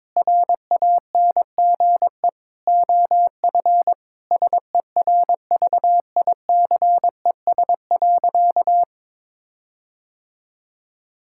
Morse code, heard as a keyed tone, about 22 wpm, 700 Hz. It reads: RANGE OF SERVICES.
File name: practice / 22wpm / Sentences with 4-Word Phrases - 22wpm